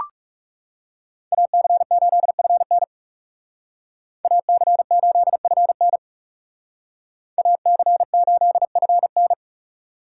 Morse code, heard as a keyed tone, about 37 wpm, 700 Hz. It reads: AC8FN AC8FN AC8FN
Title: Speed Racing - USA 2x2 Call Signs 30wpm